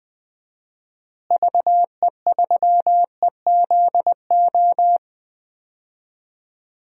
Morse code, heard as a keyed tone, about 20 wpm, 700 Hz. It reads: VE3EZO
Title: DXCC-100 Call Signs - 6 Characters - Repeated 3x - 20wpm